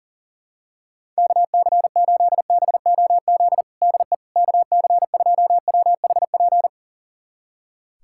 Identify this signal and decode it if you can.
Morse code, heard as a keyed tone, about 40 wpm, 700 Hz. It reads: KC8BOZ DE KC2WHP
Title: USA Callsign DE Callsign - 6 Characters - 40wpm